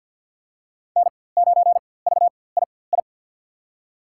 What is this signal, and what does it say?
Morse code, heard as a keyed tone, about 47 wpm, spaced as for 23 wpm, 700 Hz. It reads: N9VII
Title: USA Call Signs - 5 Characters - 4x Character Spacing - 50wpm